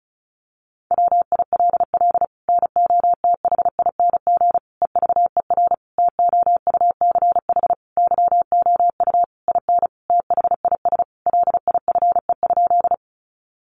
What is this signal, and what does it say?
Morse code, heard as a keyed tone, about 34 wpm, 700 Hz. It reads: WILL NOTHING EVER TOUCH YOU IN THIS LIFE?
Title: Sentences from Top 1200 Words 35wpm